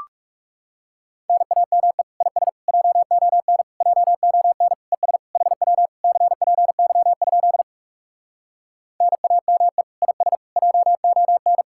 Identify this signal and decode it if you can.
Morse code, heard as a keyed tone, about 40 wpm, 700 Hz. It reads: NAME IS JON JON ES HW CPY? NAME IS JON